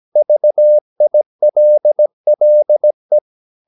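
Morse code, heard as a keyed tone, about 17 wpm, 600 Hz. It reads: VILLE
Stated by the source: Slow (QRS) Morse practice at 17 wpm, from CW med Gnister.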